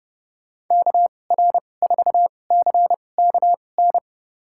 Morse code, heard as a keyed tone, about 30 wpm, spaced as for 22 wpm, 700 Hz. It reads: KR4CKN
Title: USA Call Signs - 2x Character Spacing - 30wpm